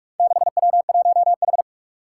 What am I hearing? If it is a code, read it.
Morse code, heard as a keyed tone, about 44 wpm, 700 Hz. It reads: 6W1H
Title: Four Challenging Letter-Number Combinations 45wpm